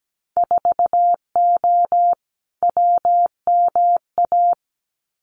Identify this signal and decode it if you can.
Morse code, heard as a keyed tone, about 17 wpm, 700 Hz. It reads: 4O WMA
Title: Winter Field Day Exchanges Encore 17wpm